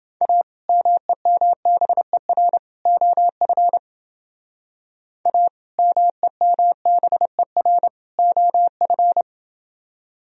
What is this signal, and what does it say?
Morse code, heard as a keyed tone, about 29 wpm, 700 Hz. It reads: A MEMBER OF A MEMBER OF